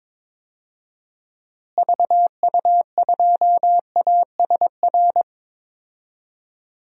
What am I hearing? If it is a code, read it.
Morse code, heard as a keyed tone, about 22 wpm, 700 Hz. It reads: VU2ASR
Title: DXCC-100 Call Signs - 6 Characters - Repeated 2x - 22wpm